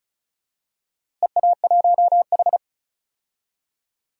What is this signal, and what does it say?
Morse code, heard as a keyed tone, about 34 wpm, 700 Hz. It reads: EA1H
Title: DXCC-100 Call Signs - 4 Characters - Repeated 3x - 35wpm